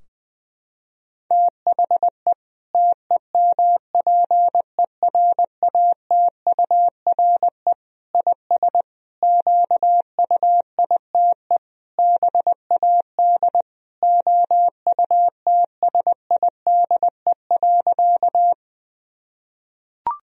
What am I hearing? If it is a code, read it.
Morse code, heard as a keyed tone, about 20 wpm, 700 Hz. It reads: THE TEMPERATURE IS QUITE BAD OUTSIDE.
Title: Sentences from Top 1000 Words - Review Part 4 - 20wpm